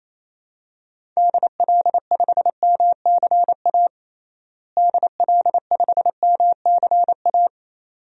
Morse code, heard as a keyed tone, about 28 wpm, 700 Hz. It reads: DL5MCA DL5MCA
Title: DXCC-100 Call Signs - 6 Characters - Repeated 2x - 28wpm